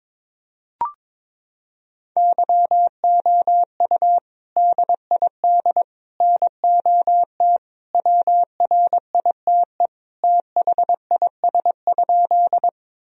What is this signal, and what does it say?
Morse code, heard as a keyed tone, about 22 wpm, 700 Hz. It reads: YOU DID NOT WRITE THIS?
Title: Sentences from Top 1000 Words - Review Part 1 - 22wpm